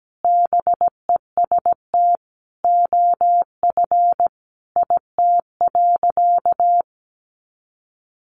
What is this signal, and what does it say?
Morse code, heard as a keyed tone, about 17 wpm, 700 Hz. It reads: BEST OF IT.
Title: Sentences from Top 400 Words 17wpm